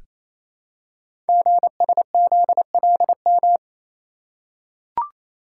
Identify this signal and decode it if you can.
Morse code, heard as a keyed tone, about 28 wpm, 700 Hz. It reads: GSZLM
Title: Code Groups - 5 Letters - 28wpm